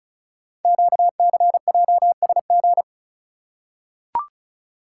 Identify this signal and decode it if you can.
Morse code, heard as a keyed tone, about 36 wpm, 690 Hz. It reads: QCJSG
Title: Code Groups - 5 Letters - 35wpm